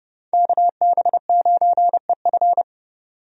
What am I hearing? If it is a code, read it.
Morse code, heard as a keyed tone, about 30 wpm, 700 Hz. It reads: KB9EF
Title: USA Call Signs - 5 Characters - 30wpm